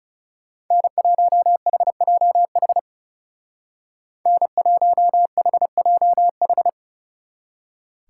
Morse code, heard as a keyed tone, about 37 wpm, 700 Hz. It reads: N1HJH N1HJH
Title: Speed Racing - USA 1x3 Call Signs 30wpm